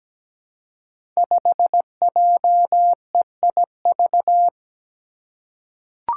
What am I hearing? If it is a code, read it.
Morse code, heard as a keyed tone, about 17 wpm, 700 Hz. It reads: 5JEIV
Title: Five Challenging Letter-Number Combinations 17wpm